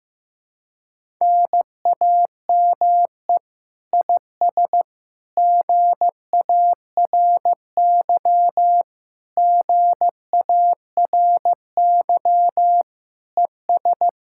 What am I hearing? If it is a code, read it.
Morse code, heard as a keyed tone, about 15 wpm, 700 Hz. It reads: NAME IS GARY GARY ES
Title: Speed-Racing Common QSO Phrases Encore - 15wpm